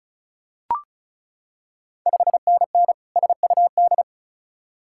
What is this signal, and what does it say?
Morse code, heard as a keyed tone, about 36 wpm, 690 Hz. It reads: 5NN SUD
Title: Ontario QSO Party Exchange 35wpm